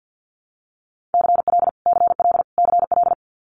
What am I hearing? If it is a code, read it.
Morse code, heard as a keyed tone, about 50 wpm, 700 Hz. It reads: CL CL CL
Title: Multi-letter Phonemes Repeated 3x 50wpm